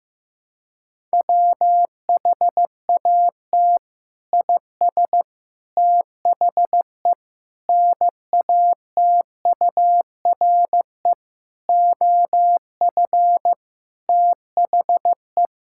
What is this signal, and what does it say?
Morse code, heard as a keyed tone, about 14 wpm, 700 Hz. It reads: WHAT IS THE NATURE OF THE
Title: Sentences from Top 1000 Words - Review Part 5 - 15wpm